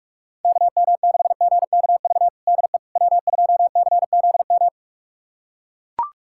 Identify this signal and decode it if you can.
Morse code, heard as a keyed tone, about 44 wpm, 700 Hz. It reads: KM6GKV DE W2CZM
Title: USA Callsign DE Callsign 45wpm